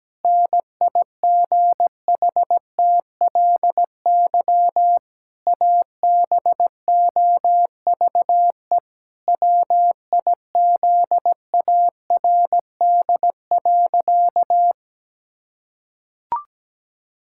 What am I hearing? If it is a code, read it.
Morse code, heard as a keyed tone, about 17 wpm, 700 Hz. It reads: NIGHTLY ABOVE WIZARD.
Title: Pangram Challenge v2 - 17wpm